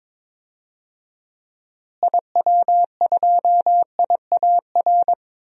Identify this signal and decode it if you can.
Morse code, heard as a keyed tone, about 22 wpm, 700 Hz. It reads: IW2IAR
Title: DXCC-100 Call Signs - 6 Characters - Repeated 2x - 22wpm